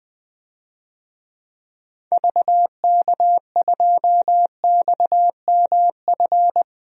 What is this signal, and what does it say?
Morse code, heard as a keyed tone, about 20 wpm, 700 Hz. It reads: VK2XMF
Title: DXCC-100 Call Signs - 6 Characters - 20wpm